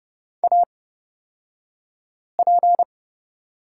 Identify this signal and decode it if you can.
Morse code, heard as a keyed tone, about 30 wpm, spaced as for 5 wpm, 700 Hz. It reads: AP
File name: Lesson 38d - Letter q - Two Characters 30wpm